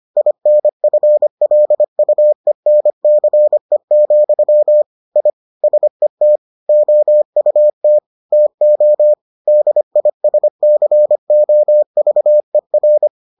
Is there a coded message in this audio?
Yes: Morse code from 0.2 to 13.1 s, heard as a keyed tone, about 25 wpm, 600 Hz. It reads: INFLUENCE, I SET OUT TO DISCOVER